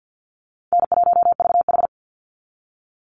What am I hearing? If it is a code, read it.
Morse code, heard as a keyed tone, about 50 wpm, 700 Hz. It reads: N1VH
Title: Easy Call Signs - 50wpm